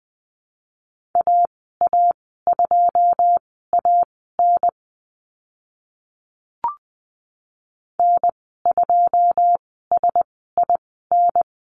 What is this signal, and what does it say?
Morse code, heard as a keyed tone, about 20 wpm, spaced as for 14 wpm, 700 Hz. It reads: AA2AN N2SIN
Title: 082 Lesson 14g - Number 2 - Call Signs 20wpm